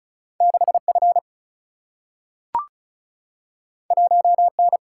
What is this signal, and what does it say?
Morse code, heard as a keyed tone, about 36 wpm, 700 Hz. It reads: 6F 1N
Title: Single-Letter Single-Number 35wpm